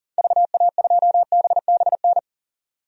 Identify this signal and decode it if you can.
Morse code, heard as a keyed tone, about 40 wpm, 700 Hz. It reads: VA2BBN
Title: DXCC-100 Call Signs - 6 Characters - 40wpm